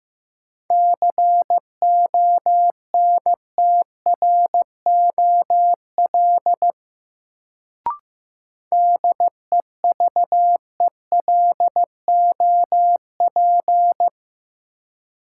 Morse code, heard as a keyed tone, about 15 wpm, 700 Hz. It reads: CONTROL DEVELOP
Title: Sentences from Top 1000 Words - Review Part 2 - New Word Drill - 15wpm